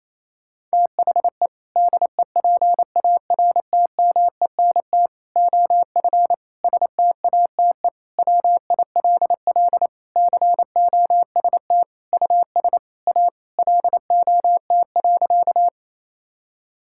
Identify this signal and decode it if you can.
Morse code, heard as a keyed tone, about 27 wpm, 700 Hz. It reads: THE DEPARTMENT OF STATE WILL COST US A LOT.
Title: Sentences from Top 1000 Words - Review Part 5 - 28wpm